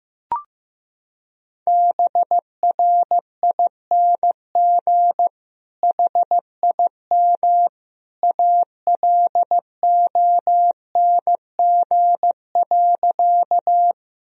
Morse code, heard as a keyed tone, about 15 wpm, 700 Hz. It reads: BRING HIM ALONG.
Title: Sentences from Top 1000 Words - Review Part 3 - 15wpm